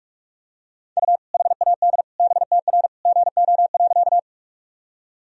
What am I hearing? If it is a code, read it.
Morse code, heard as a keyed tone, about 46 wpm, 700 Hz. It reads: U HAD BTR GO.
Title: Sentences from Top 400 Words and Abbreviations 45wpm